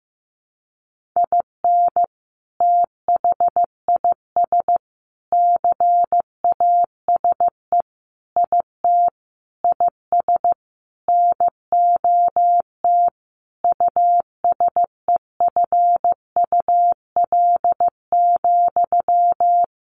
Morse code, heard as a keyed tone, about 15 wpm, 700 Hz. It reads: IN THIS CASE IT IS NOT USEFUL,
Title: Sentences from Top 1400 Words 15wpm